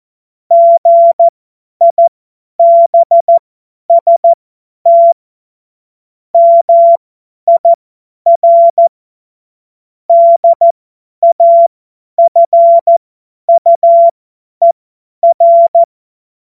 Morse code, heard as a keyed tone, about 14 wpm, spaced as for 10 wpm, 680 Hz. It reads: GIBST MIR DAFUER
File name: Morsekurs / Witze / Witz 27 Februar 2024 50 F70